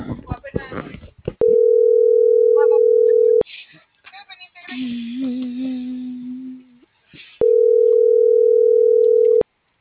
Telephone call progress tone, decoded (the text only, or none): ringback tone